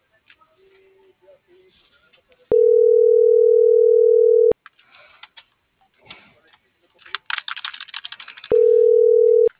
Ringback tone (North America), 440 + 480 Hz.
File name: CENTRAL MABE / out maria